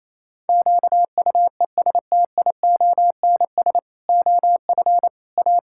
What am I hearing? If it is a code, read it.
Morse code, heard as a keyed tone, about 28 wpm, 700 Hz. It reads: QUESTIONS OF A
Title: Pangram Challenge v2 - 28wpm